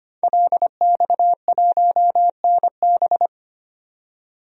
Morse code, heard as a keyed tone, about 25 wpm, 700 Hz. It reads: LX1NB
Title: DXCC-100 Call Signs - 5 Characters - 25wpm